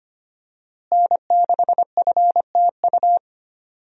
Morse code, heard as a keyed tone, about 25 wpm, 700 Hz. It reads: N6FTU
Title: USA Call Signs - 5 Characters - Repeated 3x - 25wpm